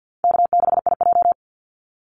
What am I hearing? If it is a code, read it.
Morse code, heard as a keyed tone, about 50 wpm, 700 Hz. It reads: K6IJ
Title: Easy Call Signs - 50wpm